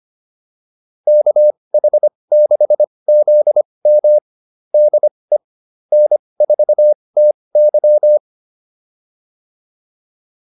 Morse code, heard as a keyed tone, about 25 wpm, spaced as for 20 wpm, 600 Hz. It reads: KH6ZM DE N4TY
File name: QSO109a 20